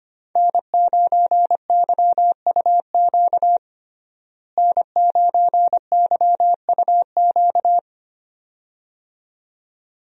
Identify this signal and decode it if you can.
Morse code, heard as a keyed tone, about 25 wpm, 700 Hz. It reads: N9YUQ N9YUQ
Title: USA Call Signs - 5 Characters - Repeated 2x - 25wpm